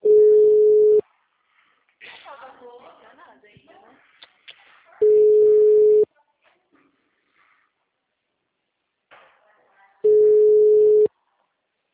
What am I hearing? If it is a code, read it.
Ringback tone (Europe, 425 Hz), with speech.